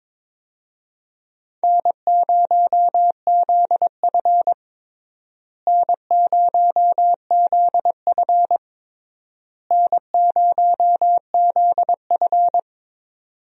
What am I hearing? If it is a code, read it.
Morse code, heard as a keyed tone, about 22 wpm, 700 Hz. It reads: N0ZF N0ZF N0ZF
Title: USA Call Signs - 4 Characters - Repeated 3x - 22wpm